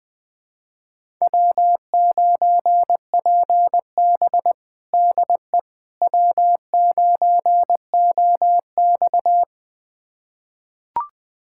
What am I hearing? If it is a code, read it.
Morse code, heard as a keyed tone, about 21 wpm, 700 Hz. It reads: W9PB DE W9OX
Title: USA Callsign DE Callsign - 4 Characters - 20wpm